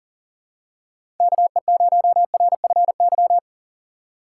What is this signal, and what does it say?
Morse code, heard as a keyed tone, about 38 wpm, 700 Hz. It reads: KE0RFY
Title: USA Call Signs - 6 Characters - 3x Character Spacing - 40wpm